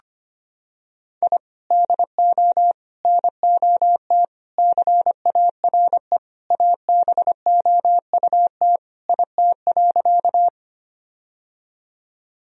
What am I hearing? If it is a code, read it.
Morse code, heard as a keyed tone, about 25 wpm, 700 Hz. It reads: I DO NOT CARE ABOUT IT.
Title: Sentences from Top 1000 Words - Review Part 1 - 25wpm